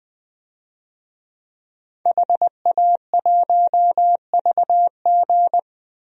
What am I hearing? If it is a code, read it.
Morse code, heard as a keyed tone, about 20 wpm, 700 Hz. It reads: HA1VG
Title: DXCC-100 Call Signs - 5 Characters - 20wpm